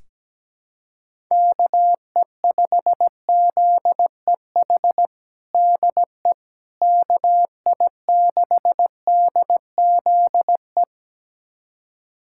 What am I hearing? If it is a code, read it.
Morse code, heard as a keyed tone, about 17 wpm, 700 Hz. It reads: KE5ZEH DE KI6DZE
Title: USA Callsign DE Callsign 17wpm